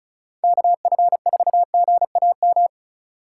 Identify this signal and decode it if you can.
Morse code, heard as a keyed tone, about 34 wpm, 700 Hz. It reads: KF4GAM
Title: USA Call Signs - 4x Character Spacing - 35wpm